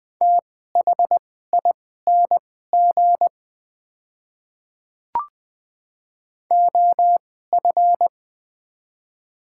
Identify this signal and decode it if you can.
Morse code, heard as a keyed tone, about 20 wpm, spaced as for 14 wpm, 700 Hz. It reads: THING OF
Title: 2x Character Spacing - Top 100 Words - 20wpm